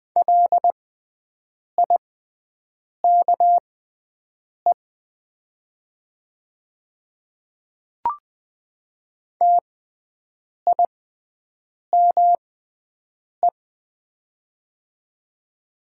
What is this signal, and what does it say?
Morse code, heard as a keyed tone, about 20 wpm, spaced as for 7 wpm, 700 Hz. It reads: LIKE TIME
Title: 6x Character Spacing - Top 100 Words - 20wpm